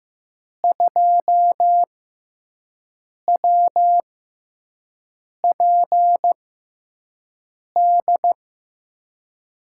Morse code, heard as a keyed tone, about 15 wpm, spaced as for 5 wpm, 700 Hz. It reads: 2WPD
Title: USA Call Signs - 6 Characters - 6x Character Spacing - 15wpm